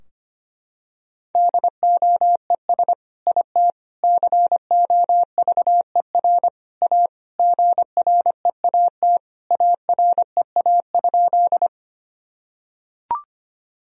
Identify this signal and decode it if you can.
Morse code, heard as a keyed tone, about 25 wpm, 700 Hz. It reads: DOES IT COVER A GREAT AREA?
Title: Sentences from Top 1000 Words - Review Part 2 - 25wpm